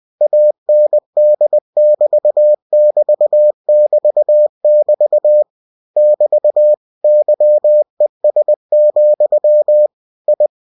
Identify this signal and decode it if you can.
Morse code, heard as a keyed tone, about 20 wpm, 600 Hz. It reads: AND <BT> <BT> <BT> <BT> <BT> YES, I